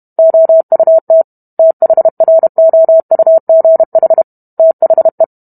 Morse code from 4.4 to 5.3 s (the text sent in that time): THE